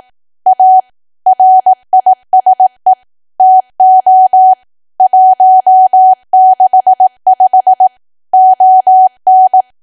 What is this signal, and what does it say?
Morse code, heard as a keyed tone, about 18 wpm, 750 Hz. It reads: A RISE TO 165 ON
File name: cwa / W1AW-Level2 / 18WPM